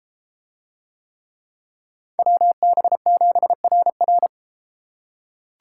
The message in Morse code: WB7RR

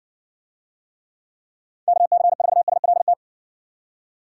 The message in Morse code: KC4SLT

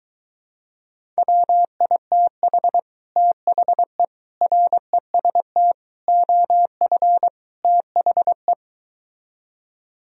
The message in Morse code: WITH THE REST OF THE

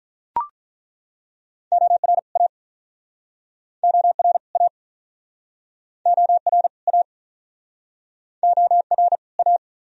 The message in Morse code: OR A OR A OR A OR A

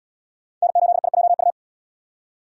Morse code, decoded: N8IJU

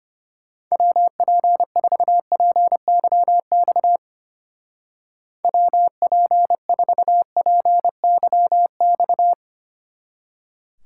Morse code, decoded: WP4PYX WP4PYX

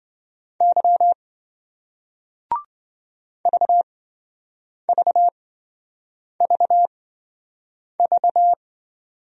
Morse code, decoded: Y VVVV

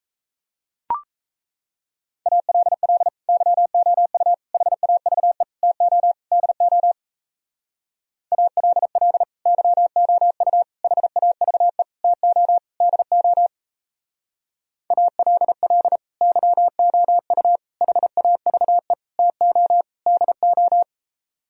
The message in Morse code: ALL YOU HAVE TO DO ALL YOU HAVE TO DO ALL YOU HAVE TO DO

